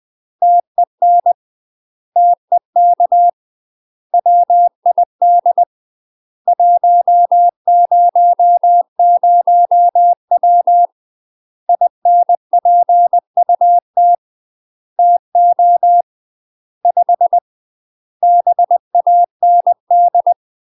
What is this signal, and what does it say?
Morse code, heard as a keyed tone, about 20 wpm, 700 Hz. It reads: TEN TEK WID 100W INPUT TO 5 BAND
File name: UK MASTER QSOs TRAINING / 20WPM / UK MASTER QSOs 0066